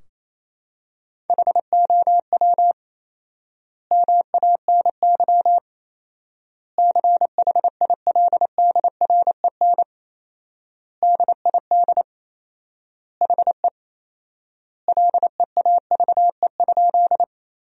Morse code, heard as a keyed tone, about 28 wpm, 700 Hz. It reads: HOW MANY CHILDREN DID HE LEAVE?
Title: 4x Word-Spacing - Sentences from Top 400 Words - 28wpm